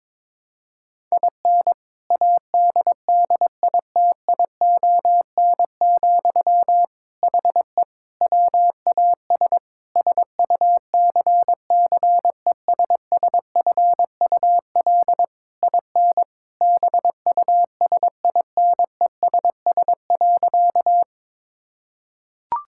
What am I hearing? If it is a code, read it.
Morse code, heard as a keyed tone, about 22 wpm, 700 Hz. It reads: IN ADDITION, HE WAS SUCCESSFUL IN BUSINESS.
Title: Sentences from Top 1000 Words - Review Part 6 - 22wpm